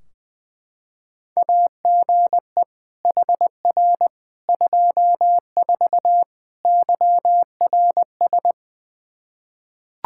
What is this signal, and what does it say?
Morse code, heard as a keyed tone, about 20 wpm, 700 Hz. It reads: AGE HR 24 YRS
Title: Practice QSO - Exchange 3 - More Info - 20wpm